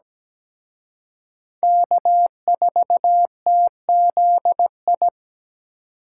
Morse code, heard as a keyed tone, about 20 wpm, 700 Hz. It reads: K4TZI
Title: Speed Racing - USA 1x3 Call Signs 17wpm